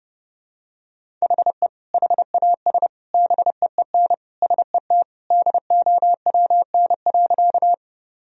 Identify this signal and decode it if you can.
Morse code, heard as a keyed tone, about 30 wpm, 700 Hz. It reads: HE HAS BEEN SET DOWN.
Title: Sentences from Top 1000 Words - Review Part 1 - 30wpm